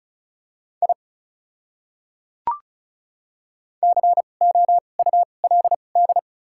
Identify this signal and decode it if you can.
Morse code, heard as a keyed tone, about 36 wpm, spaced as for 25 wpm, 700 Hz. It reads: I COULD
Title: 2x Character Spacing - Top 100 Words - 35wpm